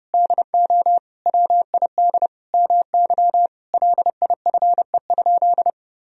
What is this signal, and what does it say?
Morse code, heard as a keyed tone, about 32 wpm, 700 Hz. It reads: DO WID MY LIFE?
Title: Sentences from Top 300 Words and Abbreviations 30wpm